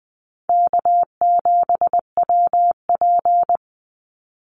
Morse code, heard as a keyed tone, about 20 wpm, 700 Hz. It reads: K7WP
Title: Easy Call Signs Repeated 2x - 20wpm